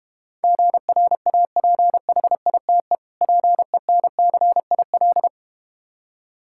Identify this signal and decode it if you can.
Morse code, heard as a keyed tone, about 32 wpm, 700 Hz. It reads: GRAPHITE PENCIL